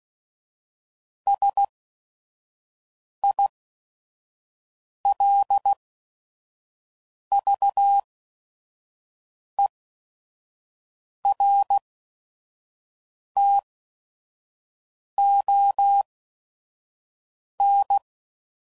Morse code, heard as a keyed tone, about 16 wpm, spaced as for 5 wpm, 800 Hz. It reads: SILVERTON
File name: CW Code Groups @5WPM / fromSM's / Cities